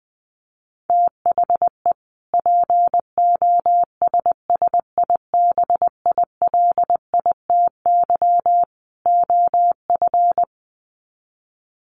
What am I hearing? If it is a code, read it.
Morse code, heard as a keyed tone, about 20 wpm, 700 Hz. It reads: THE POSSIBILITY OF
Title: Sets of 3 Words 20wpm